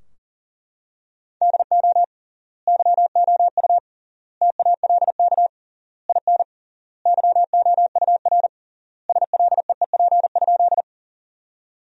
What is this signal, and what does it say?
Morse code, heard as a keyed tone, about 40 wpm, 700 Hz. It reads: DO YOU TALK IN YOUR SLEEP?
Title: 3x Word-Spacing - Sentences from Top 500 Words - 40wpm